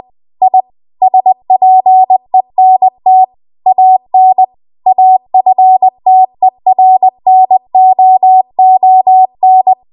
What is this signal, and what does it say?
Morse code, heard as a keyed tone, about 20 wpm, 750 Hz. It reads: I SPENT AN AFTERNOON